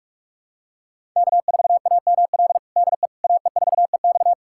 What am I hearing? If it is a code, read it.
Morse code, heard as a keyed tone, about 47 wpm, 700 Hz. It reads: K4AML DE AE4EX
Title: USA Callsign DE Callsign - 5 Characters - 45wpm